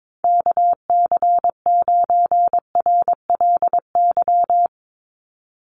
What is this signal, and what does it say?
Morse code, heard as a keyed tone, about 22 wpm, 700 Hz. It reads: KC9RLY
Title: USA Call Signs 22wpm